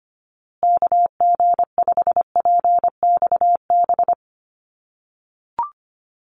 Morse code, heard as a keyed tone, about 25 wpm, 700 Hz. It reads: KG5PXB
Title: USA Call Signs Repeated 2x 25wpm